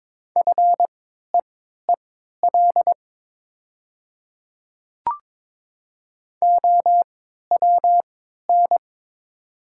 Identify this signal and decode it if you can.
Morse code, heard as a keyed tone, about 22 wpm, spaced as for 13 wpm, 700 Hz. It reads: FEEL OWN
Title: 3x Character Spacing - Top 500 Words - 22wpm